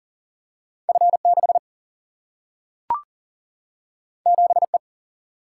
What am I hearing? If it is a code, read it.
Morse code, heard as a keyed tone, about 40 wpm, 700 Hz. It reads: F 6 7 E
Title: Single-Letter Single-Number 40wpm